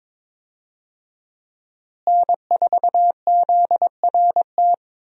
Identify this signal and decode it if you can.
Morse code, heard as a keyed tone, about 21 wpm, 700 Hz. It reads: N4ZRT
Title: USA Call Signs - 5 Characters - Repeated 3x - 22wpm